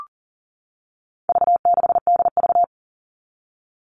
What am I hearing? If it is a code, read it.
Morse code, heard as a keyed tone, about 40 wpm, 700 Hz. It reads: V6DV
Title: All Cut Numbers - 4-Digit 40wpm